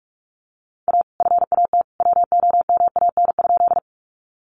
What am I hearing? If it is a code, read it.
Morse code, heard as a keyed tone, about 44 wpm, 700 Hz. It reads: A FAT WOMAN?
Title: Sentences from Top 1100 Words 45wpm